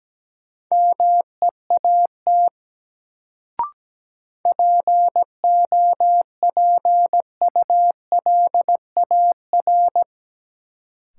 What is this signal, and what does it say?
Morse code, heard as a keyed tone, about 17 wpm, 700 Hz. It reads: MEAT POPULAR
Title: Sentences from Top 1000 Words - Review Part 3 - New Word Drill - 17wpm